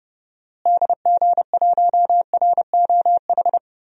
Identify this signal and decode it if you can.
Morse code, heard as a keyed tone, about 30 wpm, 700 Hz. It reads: DG1ROH
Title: DXCC-100 Call Signs - 6 Characters - Repeated 2x - 30wpm